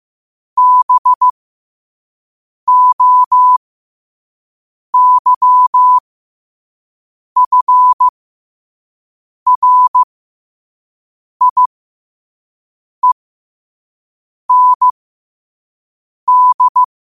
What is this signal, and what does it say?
Morse code, heard as a keyed tone, about 15 wpm, spaced as for 5 wpm, 1000 Hz. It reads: BOYFRIEND
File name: qotdfiles / 5wpm